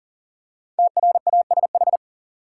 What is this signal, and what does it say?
Morse code, heard as a keyed tone, about 40 wpm, 700 Hz. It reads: TRASH
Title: Five Letter Words - 40wpm